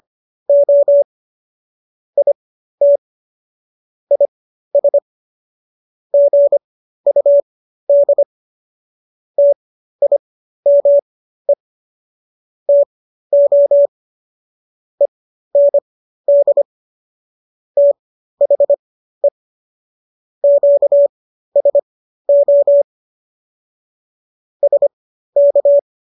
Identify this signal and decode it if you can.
Morse code, heard as a keyed tone, about 25 wpm, spaced as for 13 wpm, 570 Hz. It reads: O IT IS GUD TIME TO END THE QSO SK